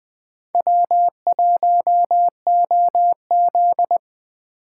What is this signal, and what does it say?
Morse code, heard as a keyed tone, about 20 wpm, 700 Hz. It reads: W1OZ
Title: Speed Racing - USA 1x2 Call Signs 17wpm